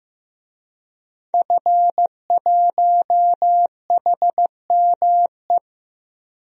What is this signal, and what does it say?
Morse code, heard as a keyed tone, about 15 wpm, 700 Hz. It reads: F1HME
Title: DXCC-100 Call Signs - 5 Characters - Repeated 3x - 15wpm